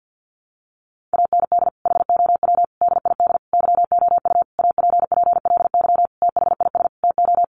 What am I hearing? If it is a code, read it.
Morse code, heard as a keyed tone, about 50 wpm, 700 Hz. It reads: AND HOW DID YOU APPLY THIS TO
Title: Sentences from Top 1200 Words 50wpm